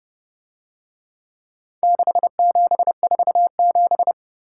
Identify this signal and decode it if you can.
Morse code, heard as a keyed tone, about 31 wpm, 700 Hz. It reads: 6747